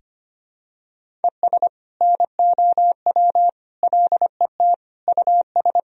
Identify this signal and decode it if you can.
Morse code, heard as a keyed tone, about 25 wpm, 700 Hz. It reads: ES NOW LET US
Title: Sentences from Top 300 Words and Abbreviations 25wpm